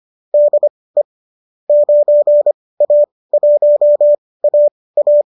Morse code, heard as a keyed tone, about 25 wpm, spaced as for 18 wpm, 590 Hz. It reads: DE 9A1AA